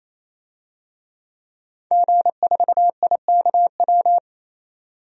Morse code, heard as a keyed tone, about 27 wpm, 700 Hz. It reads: G4IKW
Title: DXCC-100 Call Signs - 5 Characters - 28wpm